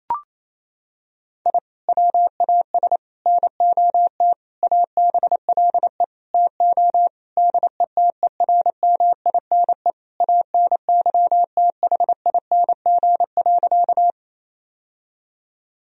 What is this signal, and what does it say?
Morse code, heard as a keyed tone, about 27 wpm, 700 Hz. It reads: I WAS NOT ABLE TO DETERMINE ANYTHING.
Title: Sentences from Top 1000 Words - Review Part 4 - 28wpm